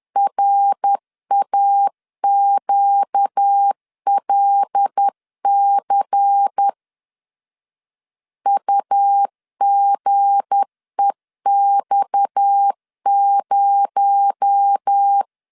Morse code, heard as a keyed tone, about 10 wpm, 800 Hz. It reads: RAQLC UGEX0
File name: LEZIONI / 7 GF9Z / 7 G 10wpm 2501